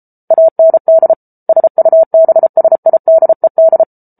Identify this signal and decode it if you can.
Morse code, heard as a keyed tone, about 34 wpm, 650 Hz. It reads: AND SUBSIDED